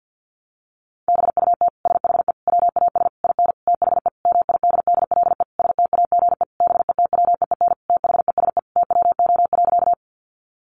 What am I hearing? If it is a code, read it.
Morse code, heard as a keyed tone, about 50 wpm, 700 Hz. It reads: BUT SHE WAS IN THE MIDDLE STAGE BETWEEN THESE TWO.
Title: Sentences from Top 1200 Words 50wpm